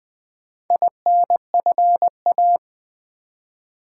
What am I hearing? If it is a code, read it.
Morse code, heard as a keyed tone, about 20 wpm, 700 Hz. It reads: INFA
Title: Build A Word - 2 Letters At A Time - 8-Letter Words - 20wpm